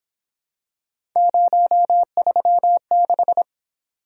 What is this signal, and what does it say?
Morse code, heard as a keyed tone, about 26 wpm, 700 Hz. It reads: 036
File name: Speed-Racing - Three Letter-Number Combinations - 22wpm